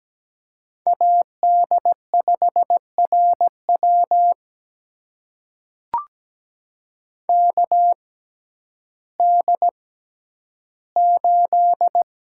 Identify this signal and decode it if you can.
Morse code, heard as a keyed tone, about 17 wpm, spaced as for 10 wpm, 700 Hz. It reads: AD5RW KD8
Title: USA Call Signs - 6x Character Spacing - 17wpm